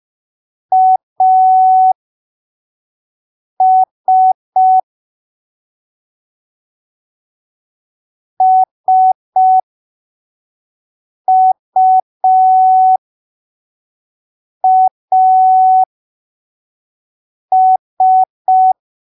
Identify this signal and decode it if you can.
Morse code, heard as a keyed tone, about 5 wpm, spaced as for 3 wpm, 750 Hz. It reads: AS SUAS
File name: arquivos / texto / texto 23 05ppm